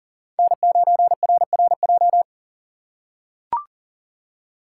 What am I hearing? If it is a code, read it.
Morse code, heard as a keyed tone, about 40 wpm, 700 Hz. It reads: N9RRJ
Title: USA Call Signs - 5 Characters - 2x Character Spacing - 40wpm